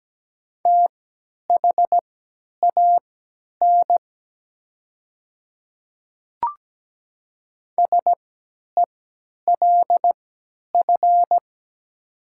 Morse code, heard as a keyed tone, about 17 wpm, spaced as for 10 wpm, 700 Hz. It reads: THAN SELF
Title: 3x Character Spacing - Top 200 Words - 17wpm